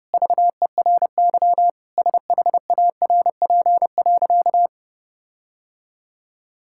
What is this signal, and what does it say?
Morse code, heard as a keyed tone, about 32 wpm, 700 Hz. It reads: VERY SHARP.
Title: Sentences from Top 1600 Words 30wpm